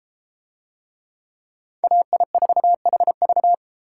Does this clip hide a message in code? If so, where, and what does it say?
Yes: Morse code from 1.8 to 3.5 s, heard as a keyed tone, about 32 wpm, 700 Hz. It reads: AI4HV